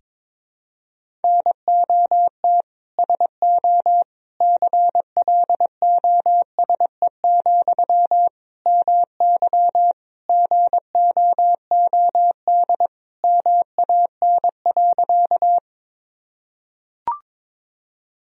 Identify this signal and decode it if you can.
Morse code, heard as a keyed tone, about 22 wpm, 700 Hz. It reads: NOT SO CLOSE, MY GOOD MAN.
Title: Sentences from Top 1000 Words - Review Part 1 - 22wpm